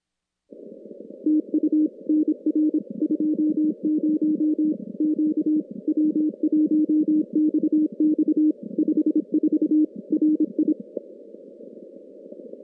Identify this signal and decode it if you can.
Morse code, heard as a keyed tone, about 26 wpm, 310 Hz. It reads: TU NR 20 Q W1XX 54 RI
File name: contests / CW-SS-k5tr- -2100